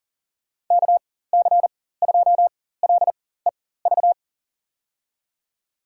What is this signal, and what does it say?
Morse code, heard as a keyed tone, about 37 wpm, spaced as for 18 wpm, 700 Hz. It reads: KC2LEV
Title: USA Call Signs - 6 Characters - 4x Character Spacing - 40wpm